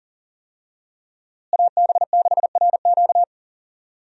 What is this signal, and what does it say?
Morse code, heard as a keyed tone, about 37 wpm, 700 Hz. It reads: AB6RQ
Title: USA Call Signs - 5 Characters - Repeated 3x - 40wpm